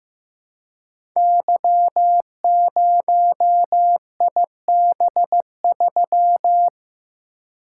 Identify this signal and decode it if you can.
Morse code, heard as a keyed tone, about 15 wpm, 700 Hz. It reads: Y0IB3
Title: Code Groups - 5 Letter:Numbers - 15wpm